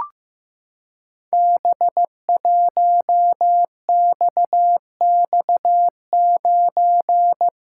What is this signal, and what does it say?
Morse code, heard as a keyed tone, about 15 wpm, 700 Hz. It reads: B1XX9
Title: Code Groups - 5 Letter:Numbers - 15wpm